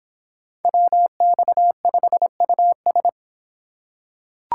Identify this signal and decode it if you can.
Morse code, heard as a keyed tone, about 25 wpm, 700 Hz. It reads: WX5US